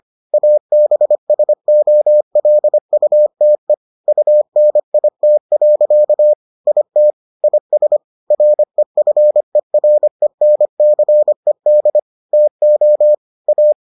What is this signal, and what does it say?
Morse code, heard as a keyed tone, about 25 wpm, 600 Hz. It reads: ABSOLUTE UNIT. IT IS REFERENCED TO A